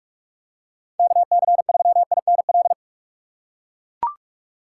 Morse code, heard as a keyed tone, about 44 wpm, 700 Hz. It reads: KC3INL
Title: USA Call Signs - 6 Characters - 6x Character Spacing - 45wpm